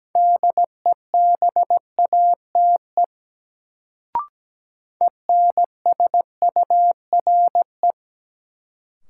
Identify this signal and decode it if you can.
Morse code, heard as a keyed tone, about 17 wpm, 700 Hz. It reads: DEBATE ENSURE
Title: Sentences from Top 1100 Words - New Word Drill - 17wpm